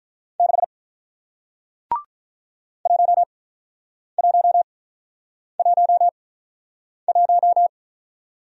Morse code, heard as a keyed tone, about 48 wpm, spaced as for 9 wpm, 700 Hz. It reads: 6 1111